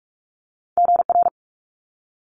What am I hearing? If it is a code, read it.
Morse code, heard as a keyed tone, about 46 wpm, 700 Hz. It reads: GR
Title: Multi-letter Phonemes Quick Fire 45wpm